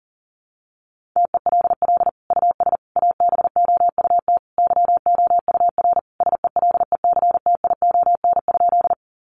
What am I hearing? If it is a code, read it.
Morse code, heard as a keyed tone, about 40 wpm, 700 Hz. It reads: TELL US ABOUT YOUR SELECTION?
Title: Sentences from Top 1400 Words 40wpm